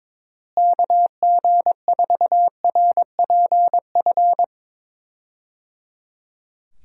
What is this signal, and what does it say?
Morse code, heard as a keyed tone, about 22 wpm, 700 Hz. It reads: KG4RPF
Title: USA Call Signs - 6 Characters - Repeated 3x - 22wpm